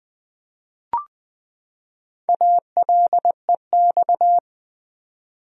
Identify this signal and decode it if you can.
Morse code, heard as a keyed tone, about 20 wpm, 700 Hz. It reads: ALEX
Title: USA Common Names - Max 5 Letters - 20wpm